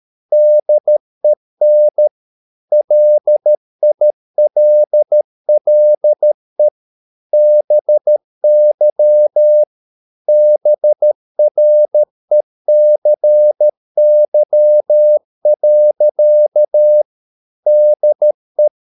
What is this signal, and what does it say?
Morse code, heard as a keyed tone, about 13 wpm, 600 Hz. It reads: DEN LILLE BY BRECY. DE
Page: Ferie 13 wpm